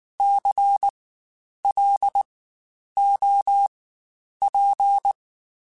Morse code, heard as a keyed tone, about 19 wpm, spaced as for 9 wpm, 800 Hz. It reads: CLOP
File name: Spaced Alphas / Spaced Alphas mp3s / Cs sp @18